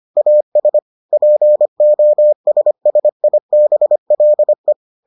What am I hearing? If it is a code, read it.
Morse code, heard as a keyed tone, about 25 wpm, 600 Hz. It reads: AS POSSIBLE